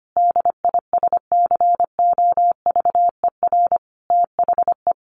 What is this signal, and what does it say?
Morse code, heard as a keyed tone, about 25 wpm, 700 Hz. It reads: DISCOVER THE
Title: Sentences from Top 1200 Words 25wpm